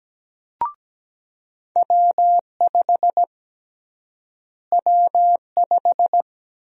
Morse code, heard as a keyed tone, about 17 wpm, 700 Hz. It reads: W5 W5
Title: USA Call Sign Prefixes - Repeated 2x - 17wpm